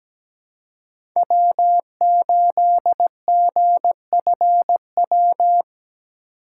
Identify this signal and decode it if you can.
Morse code, heard as a keyed tone, about 17 wpm, 700 Hz. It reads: W8GFW
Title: USA Call Signs - 5 Characters - 4x Character Spacing - 17wpm